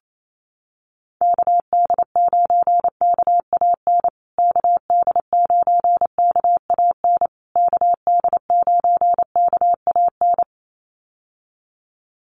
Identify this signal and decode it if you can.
Morse code, heard as a keyed tone, about 28 wpm, 700 Hz. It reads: KD9KAN KD9KAN KD9KAN
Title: USA Call Signs Repeated 3x 28wpm